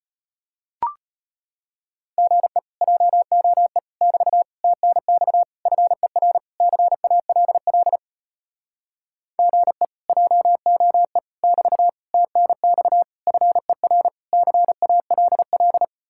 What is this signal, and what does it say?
Morse code, heard as a keyed tone, about 36 wpm, 700 Hz. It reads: GE JOE <BT> TNX FER CALL GE JOE <BT> TNX FER CALL